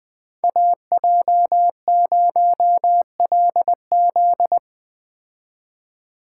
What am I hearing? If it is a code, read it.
Morse code, heard as a keyed tone, about 20 wpm, 700 Hz. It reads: AJ0LZ